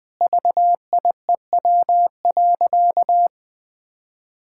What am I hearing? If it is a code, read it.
Morse code, heard as a keyed tone, about 20 wpm, 700 Hz. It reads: VIEW.
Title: Sentences from Top 1000 Words - Review Part 6 - 20wpm